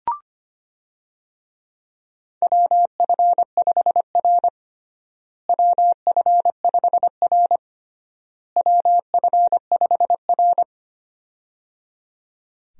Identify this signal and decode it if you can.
Morse code, heard as a keyed tone, about 25 wpm, 700 Hz. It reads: WF5R WF5R WF5R
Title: USA Call Signs - 4 Characters - Repeated 3x - 25wpm